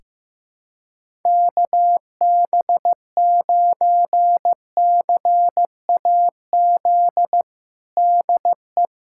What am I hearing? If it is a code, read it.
Morse code, heard as a keyed tone, about 15 wpm, 700 Hz. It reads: KB9CAZ DE
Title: USA Callsign DE Callsign 15wpm